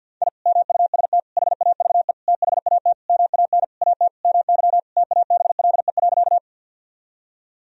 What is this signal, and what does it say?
Morse code, heard as a keyed tone, about 51 wpm, 700 Hz. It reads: I MUST HAVE THAT MAN AT MY TABLE.
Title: Sentences from Top 1000 Words - Review Part 2 - 50wpm